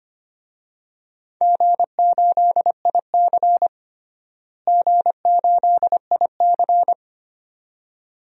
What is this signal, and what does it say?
Morse code, heard as a keyed tone, about 25 wpm, 700 Hz. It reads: G8IC G8IC
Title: DXCC-100 Call Signs - 4 Characters - Repeated 2x - 25wpm